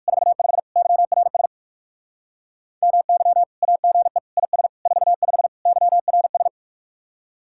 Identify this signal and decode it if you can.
Morse code, heard as a keyed tone, about 62 wpm, spaced as for 54 wpm, 700 Hz. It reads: 45 YRS MY AGE IS 45 YRS
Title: Speed-Racing Common QSO Phrases - 45wpm